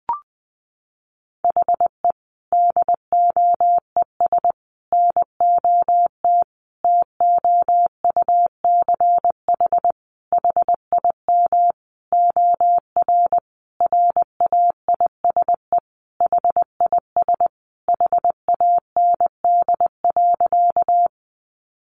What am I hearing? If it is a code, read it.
Morse code, heard as a keyed tone, about 20 wpm, 700 Hz. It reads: HE DOES NOT TOUCH HIM OR RAISE HIS HAND.
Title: Sentences from Top 1400 Words 20wpm